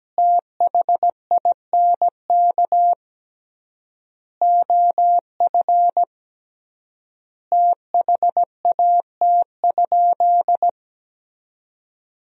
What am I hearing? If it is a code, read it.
Morse code, heard as a keyed tone, about 17 wpm, 700 Hz. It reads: THINK OF THAT?
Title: 3x Word-Spacing - Sentences from Top 200 Words - 17wpm